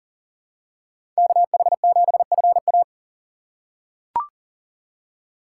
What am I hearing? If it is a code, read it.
Morse code, heard as a keyed tone, about 40 wpm, 700 Hz. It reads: KH7FA
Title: USA Call Signs - 5 Characters - 2x Character Spacing - 40wpm